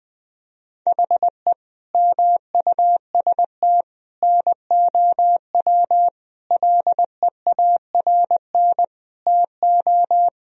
Morse code, heard as a keyed tone, about 21 wpm, 700 Hz. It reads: HE MUST NOW LEARN TO
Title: Sentences from Top 1000 Words - Review Part 1 - 20wpm